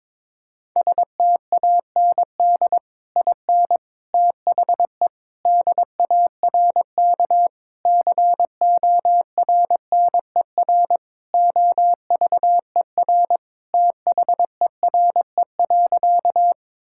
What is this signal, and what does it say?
Morse code, heard as a keyed tone, about 21 wpm, 700 Hz. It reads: STAND IN THE DARK CORNER OVER THERE.
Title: Sentences from Top 1600 Words 22wpm